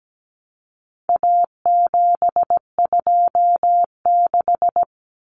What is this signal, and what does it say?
Morse code, heard as a keyed tone, about 17 wpm, 700 Hz. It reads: A726
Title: Common Cut Numbers - 4-Digit 17wpm